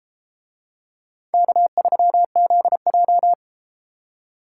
Morse code, heard as a keyed tone, about 31 wpm, 700 Hz. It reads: K3ZJ